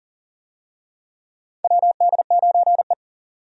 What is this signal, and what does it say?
Morse code, heard as a keyed tone, about 40 wpm, 700 Hz. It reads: WD9E
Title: USA Call Signs - 4 Characters - 40wpm